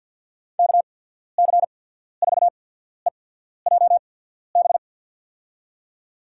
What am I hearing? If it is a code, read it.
Morse code, heard as a keyed tone, about 52 wpm, spaced as for 14 wpm, 700 Hz. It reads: KC4EJB
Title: USA Call Signs - 6 Characters - 8x Character Spacing - 50wpm